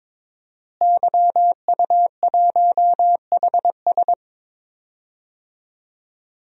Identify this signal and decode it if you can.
Morse code, heard as a keyed tone, about 22 wpm, 700 Hz. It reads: YU1HS
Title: DXCC-100 Call Signs - 5 Characters - Repeated 2x - 22wpm